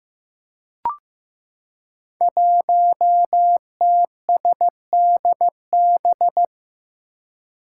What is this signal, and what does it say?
Morse code, heard as a keyed tone, about 15 wpm, 700 Hz. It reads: 1TSDB
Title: Five Challenging Letter-Number Combinations 15wpm